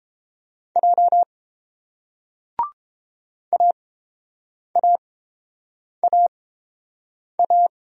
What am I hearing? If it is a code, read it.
Morse code, heard as a keyed tone, about 30 wpm, spaced as for 7 wpm, 700 Hz. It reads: J AAAA